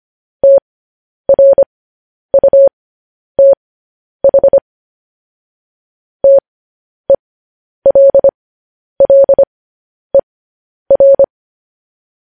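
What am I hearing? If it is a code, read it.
Morse code, heard as a keyed tone, about 25 wpm, spaced as for 10 wpm, 550 Hz. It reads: TRUTH TELLER
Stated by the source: Morse code news headlines broadcast at 10 wpm.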